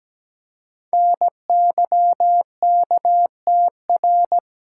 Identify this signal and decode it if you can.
Morse code, heard as a keyed tone, about 17 wpm, 700 Hz. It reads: NYKTR
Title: Code Groups - 5 Letters - 17wpm